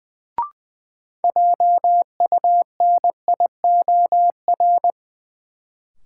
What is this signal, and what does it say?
Morse code, heard as a keyed tone, about 20 wpm, 700 Hz. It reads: JUNIOR
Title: Sentences from Top 1000 Words - Review Part 7 - New Word Drill - 20wpm